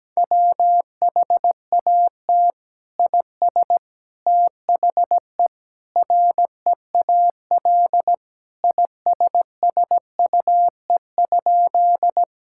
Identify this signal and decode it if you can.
Morse code, heard as a keyed tone, about 17 wpm, 700 Hz. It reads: WHAT IS THE REAL ISSUE?
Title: Sentences from Top 1000 Words - Review Part 6 - 17wpm